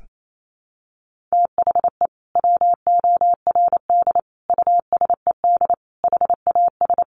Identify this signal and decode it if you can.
Morse code, heard as a keyed tone, about 28 wpm, 700 Hz. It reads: THE WORD USED HAS
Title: Sentences from Top 1100 Words 28wpm